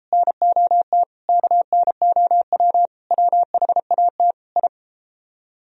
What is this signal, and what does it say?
Morse code, heard as a keyed tone, about 30 wpm, 700 Hz. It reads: NOT KNOW WHAT I